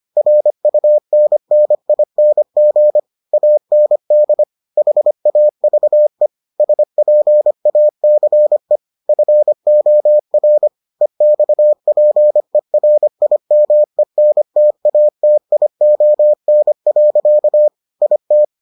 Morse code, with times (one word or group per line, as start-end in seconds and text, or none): 0.2-3.0 RUNNING
3.3-4.4 AND
4.8-6.3 HAVE
6.6-8.8 SPACE
9.1-10.7 FOR
11.0-17.7 EXPERIMENTATION.
18.0-18.4 IT